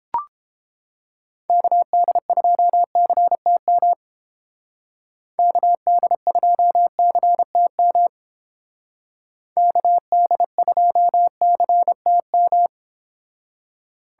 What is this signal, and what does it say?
Morse code, heard as a keyed tone, about 28 wpm, 700 Hz. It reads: KD2CTM KD2CTM KD2CTM